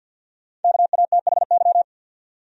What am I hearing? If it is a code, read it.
Morse code, heard as a keyed tone, about 53 wpm, 700 Hz. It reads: KATHY
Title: USA Common Names - Max 5 Letters - 50wpm